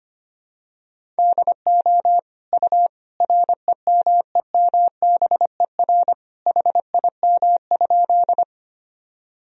Morse code, heard as a keyed tone, about 25 wpm, 700 Hz. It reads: DO U REMEMBER HIM?
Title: Sentences from Top 400 Words and Abbreviations 25wpm